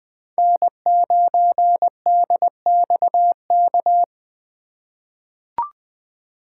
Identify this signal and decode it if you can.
Morse code, heard as a keyed tone, about 20 wpm, 700 Hz. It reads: N9DXK
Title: USA Call Signs - 5 Characters - 2x Character Spacing - 20wpm